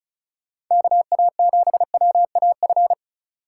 Morse code, heard as a keyed tone, about 43 wpm, 700 Hz. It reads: KA7WAF
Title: Speed Racing - USA 2x3 Call Signs 35wpm